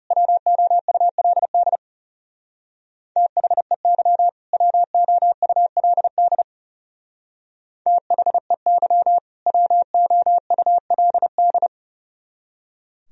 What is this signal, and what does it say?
Morse code, heard as a keyed tone, about 36 wpm, 700 Hz. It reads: WOULD THEY WOULD THEY WOULD
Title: Speed-Racing - Sets of 2 Words 30wpm